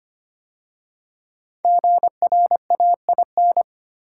Morse code, heard as a keyed tone, about 25 wpm, 700 Hz. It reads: GRAIN
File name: 5 Letter Words - First 3 Letters Given - 25wpm